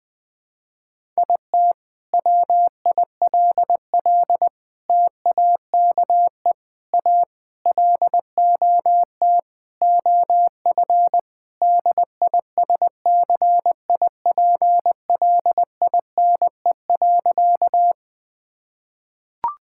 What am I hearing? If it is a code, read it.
Morse code, heard as a keyed tone, about 20 wpm, 700 Hz. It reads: IT WILL TAKE A LOT OF DISCIPLINE.
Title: Sentences from Top 1000 Words - Review Part 6 - 20wpm